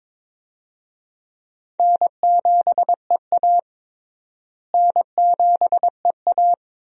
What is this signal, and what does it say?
Morse code, heard as a keyed tone, about 22 wpm, 700 Hz. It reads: N7EA N7EA
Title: USA Call Signs - 4 Characters - Repeated 2x - 22wpm